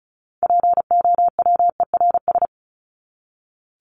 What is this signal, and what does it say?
Morse code, heard as a keyed tone, about 34 wpm, 700 Hz. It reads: POWERS
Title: ES:S Suffix - Top 100 Words with Base Words 35wpm